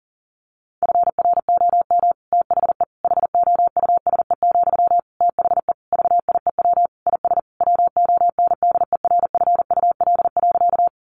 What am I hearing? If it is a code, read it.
Morse code, heard as a keyed tone, about 40 wpm, 700 Hz. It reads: FROM THE HOUSE, THE VIEW IS WONDERFUL.
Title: Sentences from Top 1100 Words 40wpm